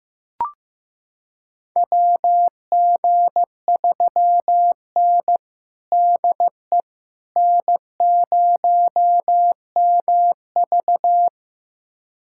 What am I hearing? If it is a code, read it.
Morse code, heard as a keyed tone, about 15 wpm, 700 Hz. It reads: WG3N DE N0MV
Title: USA Callsign DE Callsign - 4 Characters - 15wpm